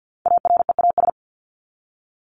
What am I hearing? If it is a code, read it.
Morse code, heard as a keyed tone, about 48 wpm, 700 Hz. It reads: AREAS
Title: ES:S Suffix - Top 100 Words with Base Words 50wpm